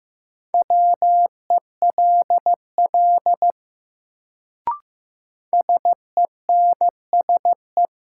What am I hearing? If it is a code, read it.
Morse code, heard as a keyed tone, about 15 wpm, 700 Hz. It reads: WELL SENSE
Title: Sets of 2 Words - New Word Drill - 15wpm